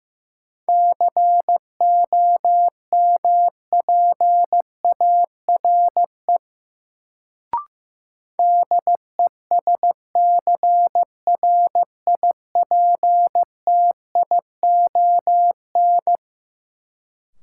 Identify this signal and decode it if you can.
Morse code, heard as a keyed tone, about 15 wpm, 700 Hz. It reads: COMPARE DESCRIPTION
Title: Sentences from Top 1300 Words - New Word Drill - 15wpm